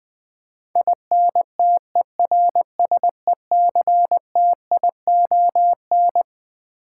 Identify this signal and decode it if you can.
Morse code, heard as a keyed tone, about 20 wpm, 700 Hz. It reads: INTERSECTION
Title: Twelve Letter Words - 20wpm